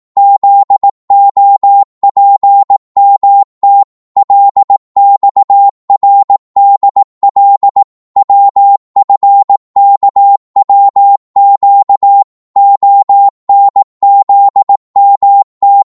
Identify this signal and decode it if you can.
Morse code, heard as a keyed tone, about 18 wpm, 800 Hz. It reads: ZOPMT LXRDL WFKWQ ONZMT
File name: Morse / ltr-18wpm